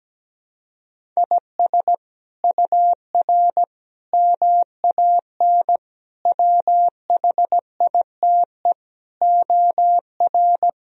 Morse code, heard as a keyed tone, about 17 wpm, 700 Hz. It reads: IS UR MAN WHITE OR
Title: Sentences from Top 400 Words and Abbreviations 17wpm